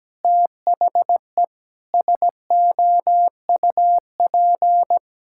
Morse code, heard as a keyed tone, about 17 wpm, 700 Hz. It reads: THE SOUP
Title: Sentences from Top 1000 Words - Review Part 6 - 17wpm